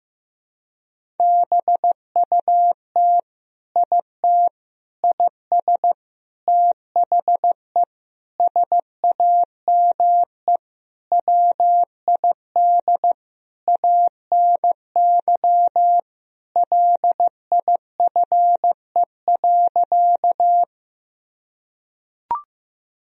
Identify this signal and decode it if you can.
Morse code, heard as a keyed tone, about 15 wpm, 700 Hz. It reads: BUT IT IS THE SAME WID ANY LIFE.
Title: Sentences from Top 300 Words and Abbreviations 15wpm